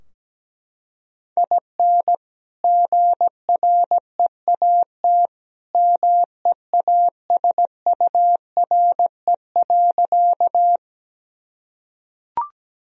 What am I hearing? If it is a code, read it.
Morse code, heard as a keyed tone, about 17 wpm, 700 Hz. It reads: IN GREAT MEASURE.
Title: Sentences from Top 1000 Words - Review Part 2 - 17wpm